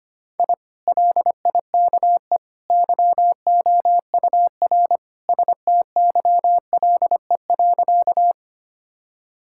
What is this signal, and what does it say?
Morse code, heard as a keyed tone, about 25 wpm, 700 Hz. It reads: I LIKE YOUR STYLE.
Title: Sentences from Top 1000 Words - Review Part 5 - 25wpm